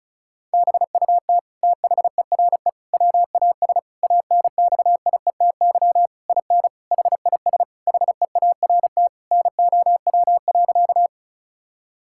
Morse code, heard as a keyed tone, about 34 wpm, 700 Hz. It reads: BUT THERE WAS ANXIETY IN HIS HEART NOW.
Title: Sentences from Top 1600 Words 35wpm